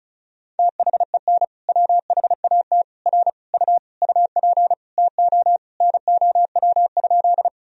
Morse code, heard as a keyed tone, about 36 wpm, 700 Hz. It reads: THEN WHAT R U UP TO NOW?
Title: Sentences from Top 100 Words and Abbreviations 35wpm